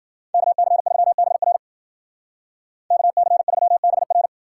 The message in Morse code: KC3BR KC3BR

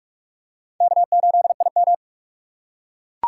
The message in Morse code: K8IM

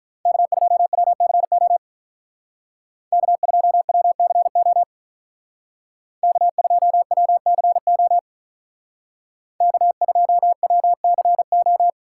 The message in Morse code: K2WCO K2WCO K2WCO K2WCO

